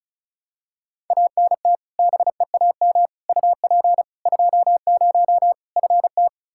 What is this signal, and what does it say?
Morse code, heard as a keyed tone, about 34 wpm, 700 Hz. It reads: ANT BEAM UP 20 FT
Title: K1USN SST QSO Practice 35wpm